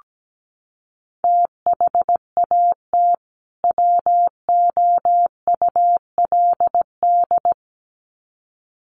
Morse code, heard as a keyed tone, about 17 wpm, 700 Hz. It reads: THAT WOULD
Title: Sets of 2 Words 17wpm